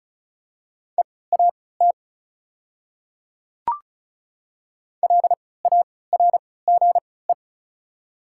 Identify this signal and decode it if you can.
Morse code, heard as a keyed tone, about 34 wpm, spaced as for 20 wpm, 700 Hz. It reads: EAT LARGE
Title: 3x Character Spacing - Top 300 Words - 35wpm